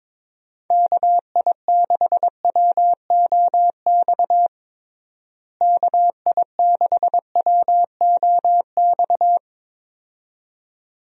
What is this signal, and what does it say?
Morse code, heard as a keyed tone, about 22 wpm, 700 Hz. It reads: KI6WOX KI6WOX
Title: USA Call Signs - 6 Characters - Repeated 2x - 22wpm